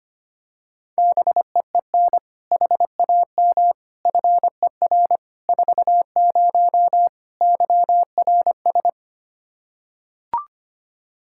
Morse code, heard as a keyed tone, about 25 wpm, 700 Hz. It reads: BEEN HAM FER 40 YRS
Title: Practice QSO - Exchange 3 - More Info - 25wpm